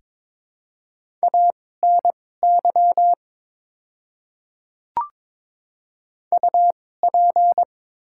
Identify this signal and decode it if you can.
Morse code, heard as a keyed tone, about 22 wpm, spaced as for 16 wpm, 700 Hz. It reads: ANY UP
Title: 2x Character Spacing - Top 300 Words - 22wpm